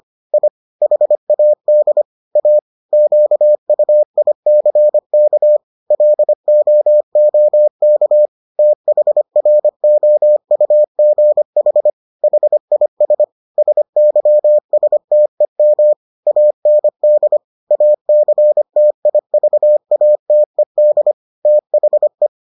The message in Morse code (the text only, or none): I HAD A QUICK LOOK THROUGH HIS SYSTEM AND ACTIVATED THE